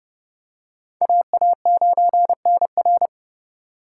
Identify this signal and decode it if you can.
Morse code, heard as a keyed tone, about 32 wpm, 700 Hz. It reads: AA9NR
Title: USA Call Signs - 4x Character Spacing - 30wpm